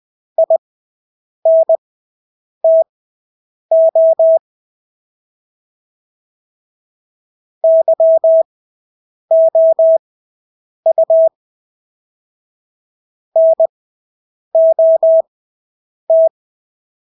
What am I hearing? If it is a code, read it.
Morse code, heard as a keyed tone, about 20 wpm, spaced as for 8 wpm, 650 Hz. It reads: INTO YOU NOT
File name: Engelske ord 20/8 wpm